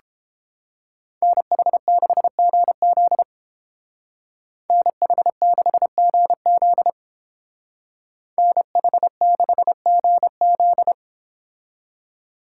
Morse code, heard as a keyed tone, about 31 wpm, 700 Hz. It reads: NH6GZ NH6GZ NH6GZ